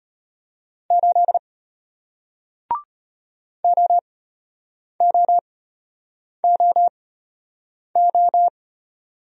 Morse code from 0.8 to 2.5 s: 8

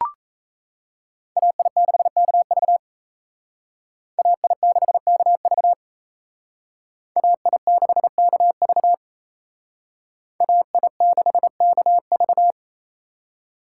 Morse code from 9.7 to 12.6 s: AI6KV

